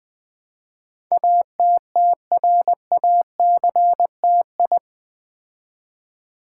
Morse code, heard as a keyed tone, about 20 wpm, 700 Hz. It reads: ATTRACTI
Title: Build A Word - 4 Letters At A Time - 10-Letter Words - 20wpm